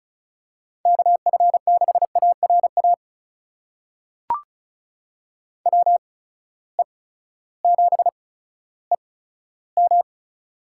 Morse code, heard as a keyed tone, about 36 wpm, spaced as for 16 wpm, 700 Hz. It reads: KF6ARA WE7EM
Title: USA Call Signs - 8x Character Spacing - 35wpm